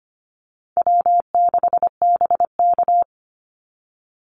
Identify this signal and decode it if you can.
Morse code, heard as a keyed tone, about 25 wpm, 700 Hz. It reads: W6BK
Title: Easy Call Signs - 25wpm